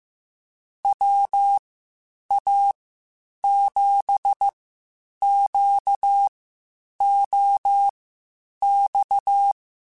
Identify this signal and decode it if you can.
Morse code, heard as a keyed tone, about 15 wpm, spaced as for 8 wpm, 800 Hz. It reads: WA7QOX
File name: (spaced) / QSOs Spaced-mp3s / WA7QOX 1 @13 sp